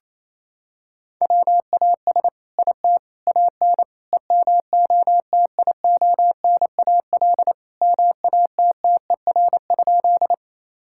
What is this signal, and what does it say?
Morse code, heard as a keyed tone, about 27 wpm, 700 Hz. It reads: WAS IT AN EMOTIONAL MATTER?
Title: Sentences from Top 1000 Words - Review Part 5 - 28wpm